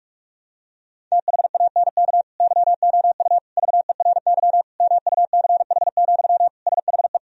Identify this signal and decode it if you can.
Morse code, heard as a keyed tone, about 46 wpm, 700 Hz. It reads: THANK YOU VERY MUCH, SHE
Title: Sentences from Top 1000 Words - Review Part 4 - 45wpm